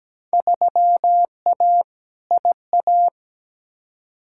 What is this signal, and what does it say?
Morse code, heard as a keyed tone, about 17 wpm, 700 Hz. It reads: 3A IA
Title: Field Day Exchanges Encore - 17wpm